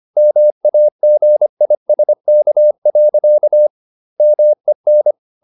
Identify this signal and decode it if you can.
Morse code, heard as a keyed tone, about 25 wpm, 600 Hz. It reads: MAGISK. MEN
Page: Ferie 25 wpm